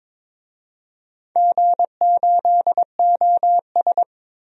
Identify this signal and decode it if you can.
Morse code, heard as a keyed tone, about 22 wpm, 700 Hz. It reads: G8OS
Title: DXCC-100 Call Signs - 4 Characters - Repeated 2x - 22wpm